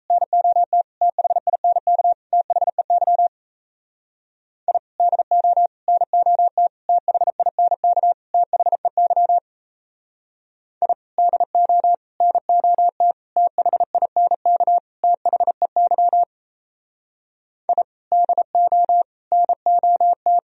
Morse code, heard as a keyed tone, about 34 wpm, 700 Hz. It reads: NOT THINK THEY I DO NOT THINK THEY I DO NOT THINK THEY I DO NOT
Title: Speed-Racing - Sets of 5 Words 28wpm